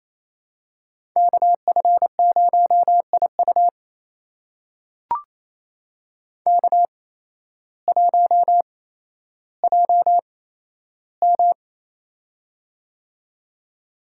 Morse code, heard as a keyed tone, about 28 wpm, spaced as for 13 wpm, 700 Hz. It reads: KF0IU K1JM
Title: USA Call Signs - 8x Character Spacing - 28wpm